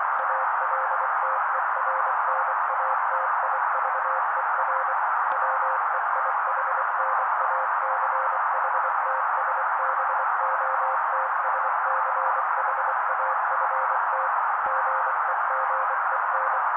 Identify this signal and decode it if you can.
Morse code, heard as a keyed tone, about 24 wpm, 530 Hz. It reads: ALTERNATIVER WEIHNACHTSBOTSCHAFT GEGEN
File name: daily / 23wpm noise